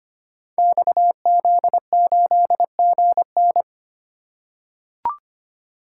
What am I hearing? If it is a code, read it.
Morse code, heard as a keyed tone, about 25 wpm, 700 Hz. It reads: XZ8GN
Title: Code Groups - 5 Letter:Numbers - 25wpm